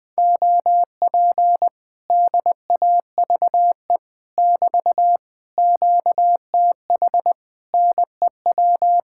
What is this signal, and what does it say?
Morse code, heard as a keyed tone, about 20 wpm, 700 Hz. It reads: OP DAVE <BT> QTH NEW
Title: Speed-Racing Common QSO Phrases Encore - 20wpm